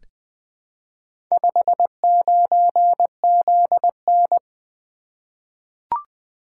Morse code, heard as a keyed tone, about 20 wpm, 700 Hz. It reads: 59ZN
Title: Four Letter-Number Combinations - 20wpm